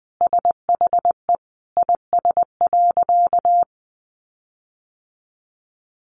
Morse code, heard as a keyed tone, about 19 wpm, 700 Hz. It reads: SHE IS.
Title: Sentences from Top 1300 Words 20wpm